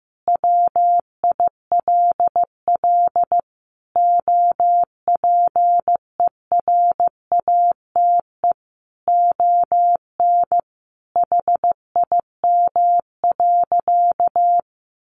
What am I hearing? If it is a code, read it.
Morse code, heard as a keyed tone, about 15 wpm, 700 Hz. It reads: WILL OPERATE ON HIM.
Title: Sentences from Top 1500 Words 15wpm